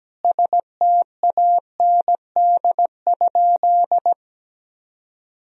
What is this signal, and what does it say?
Morse code, heard as a keyed tone, about 17 wpm, 700 Hz. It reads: STAND?
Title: Sentences from Top 1000 Words - Review Part 3 - 17wpm